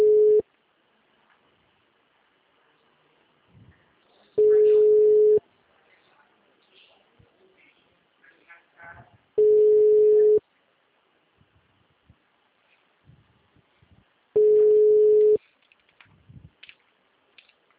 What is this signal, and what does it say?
Ringback tone (Europe, 425 Hz).
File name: out- -2021- - - .74661